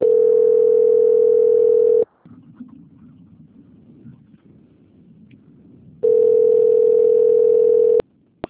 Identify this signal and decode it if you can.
Ringback tone (North America), 440 + 480 Hz.